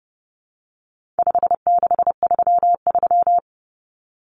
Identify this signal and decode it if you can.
Morse code, heard as a keyed tone, about 30 wpm, 700 Hz. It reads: 5633
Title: Numbers - 4-Digit 30wpm